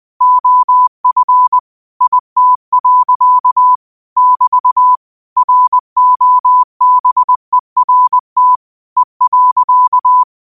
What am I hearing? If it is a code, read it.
Morse code, heard as a keyed tone, about 20 wpm, 1000 Hz. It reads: OF IT. <BT> ROBERT E.
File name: qotdfiles / 20wpm